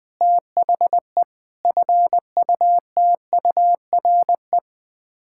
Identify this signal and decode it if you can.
Morse code, heard as a keyed tone, about 21 wpm, 700 Hz. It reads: THE FUTURE
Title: Sets of 6 Words - 20wpm